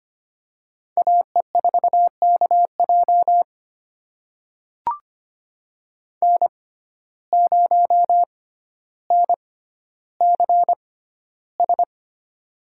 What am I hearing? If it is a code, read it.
Morse code, heard as a keyed tone, about 25 wpm, spaced as for 13 wpm, 700 Hz. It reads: AE4KJ N0NCS
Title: USA Call Signs - 5 Characters - 6x Character Spacing - 25wpm